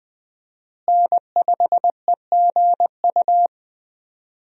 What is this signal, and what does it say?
Morse code, heard as a keyed tone, about 20 wpm, 700 Hz. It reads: N5EGU
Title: USA Call Signs - 5 Characters - Repeated 3x - 20wpm